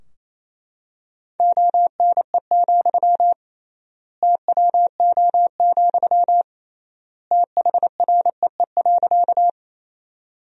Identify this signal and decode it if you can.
Morse code, heard as a keyed tone, about 28 wpm, 700 Hz. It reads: ONE, TWO, THREE.
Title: 3x Word-Spacing - Sentences from Top 200 Words - 28wpm